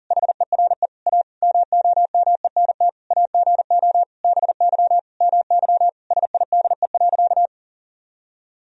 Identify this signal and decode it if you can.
Morse code, heard as a keyed tone, about 40 wpm, 700 Hz. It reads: HERE A MOMENT AGO BY MY SIDE.
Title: Sentences from Top 1000 Words - Review Part 3 - 40wpm